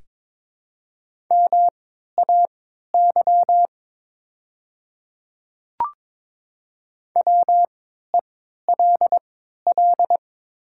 Morse code, heard as a keyed tone, about 22 wpm, spaced as for 13 wpm, 700 Hz. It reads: MAY WELL
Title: 3x Character Spacing - Top 200 Words - 22wpm